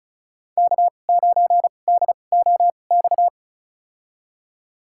Morse code, heard as a keyed tone, about 36 wpm, spaced as for 25 wpm, 700 Hz. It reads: K9DOX
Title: USA Call Signs - 5 Characters - 2x Character Spacing - 35wpm